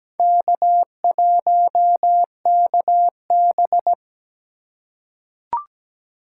Morse code, heard as a keyed tone, about 17 wpm, 700 Hz. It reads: K1KB
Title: USA Call Signs - 4 Characters - 3x Character Spacing - 17wpm